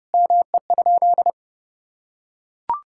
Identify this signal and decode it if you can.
Morse code, heard as a keyed tone, about 30 wpm, 700 Hz. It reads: ME?
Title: Sentences from Top 1600 Words 30wpm